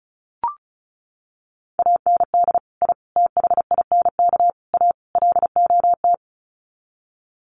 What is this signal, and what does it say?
Morse code, heard as a keyed tone, about 34 wpm, 700 Hz. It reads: AND I THINK A LOT
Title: Sets of 5 Words Encore2 35wpm